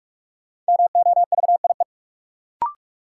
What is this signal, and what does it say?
Morse code, heard as a keyed tone, about 41 wpm, 700 Hz. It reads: MOVIE
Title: Sentences from Top 1000 Words - Review Part 5 - New Word Drill - 45wpm